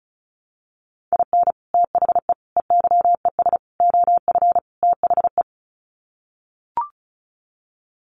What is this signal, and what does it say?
Morse code, heard as a keyed tone, about 34 wpm, 700 Hz. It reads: IN THE EYES OF THE
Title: Sets of 5 Words 35wpm